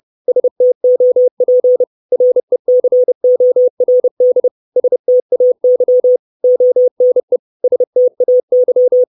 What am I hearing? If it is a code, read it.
Morse code, heard as a keyed tone, about 30 wpm, 500 Hz. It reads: STOP RECORD STAY ONE STAY
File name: 632 Words 30WPM